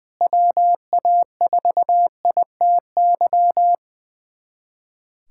Morse code, heard as a keyed tone, about 21 wpm, 700 Hz. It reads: WA4ITY
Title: USA Callsign DE Callsign - 6 Characters - 20wpm